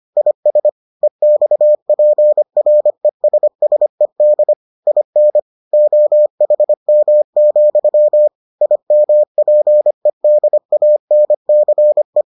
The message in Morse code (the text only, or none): IS EXPRESSED IN OHM, IMPEDANCE